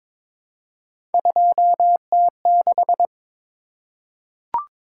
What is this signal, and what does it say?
Morse code, heard as a keyed tone, about 22 wpm, 700 Hz. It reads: 2T6
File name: practice / 22wpm / Three Letter-Number Combinations - 22wpm